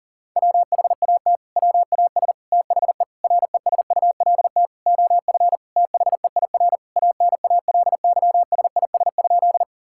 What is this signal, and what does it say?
Morse code, heard as a keyed tone, about 40 wpm, 700 Hz. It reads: WHAT WAS THE RESULT OF THEIR ANALYSIS?
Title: Sentences from Top 1000 Words - Review Part 5 - 40wpm